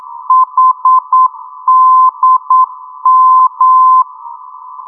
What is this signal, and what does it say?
Morse code, heard as a keyed tone, about 9 wpm, 1030 Hz. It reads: HDM
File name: europe / d-8023 - ndb coleman hdm-334 (2004) (rainer brannolte, d) xc